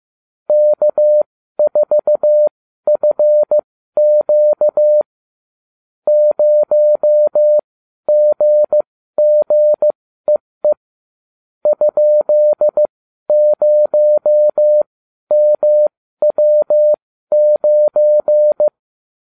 Morse code, heard as a keyed tone, about 15 wpm, spaced as for 12 wpm, 600 Hz. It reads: K4FQ 0GGEE ?0MW9